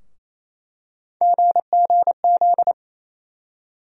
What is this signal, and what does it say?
Morse code, heard as a keyed tone, about 27 wpm, 700 Hz. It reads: GGZ
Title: USA Call Sign Postfixes - 3 Letters 28wpm